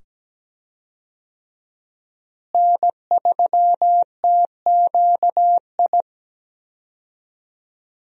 Morse code, heard as a keyed tone, about 17 wpm, 700 Hz. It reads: N3TQI
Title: USA Call Signs - 5 Characters - Repeated 2x - 17wpm